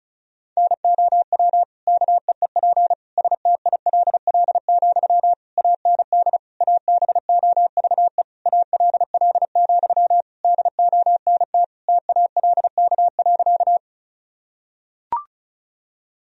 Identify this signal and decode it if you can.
Morse code, heard as a keyed tone, about 36 wpm, 700 Hz. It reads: NOW KEEP STILL, AND ABOVE ALL, DONT TALK.
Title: Sentences from Top 1000 Words - Review Part 2 - 35wpm